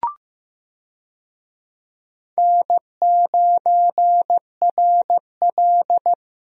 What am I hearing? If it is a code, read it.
Morse code, heard as a keyed tone, about 15 wpm, 700 Hz. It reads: N9RL
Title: USA Call Signs - 4 Characters - Repeated 2x - 15wpm